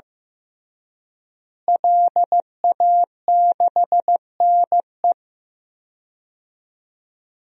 Morse code, heard as a keyed tone, about 16 wpm, 700 Hz. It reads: LA6NE
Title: DXCC-100 Call Signs - 5 Characters - Repeated 3x - 15wpm